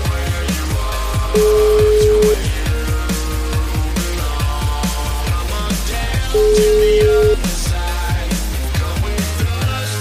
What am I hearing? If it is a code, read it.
Ringback tone (Europe, 425 Hz), with noise.